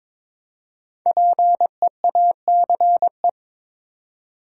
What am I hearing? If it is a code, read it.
Morse code, heard as a keyed tone, about 22 wpm, 700 Hz. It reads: PEACE
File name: 5 Letter Words - First 3 Letters Given - 22wpm